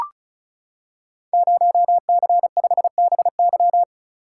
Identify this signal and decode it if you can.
Morse code, heard as a keyed tone, about 36 wpm, 700 Hz. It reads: 0C5BY
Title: Code Groups - 5 Letter:Numbers - 35wpm